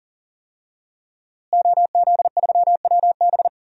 Morse code, heard as a keyed tone, about 40 wpm, 700 Hz. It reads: OZ3WB
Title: DXCC-100 Call Signs - 5 Characters - Repeated 3x - 40wpm